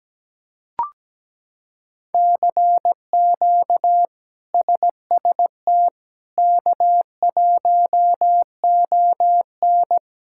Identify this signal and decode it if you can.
Morse code, heard as a keyed tone, about 17 wpm, 700 Hz. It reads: CQ SST K1ON
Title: K1USN SST QSO Practice - 17wpm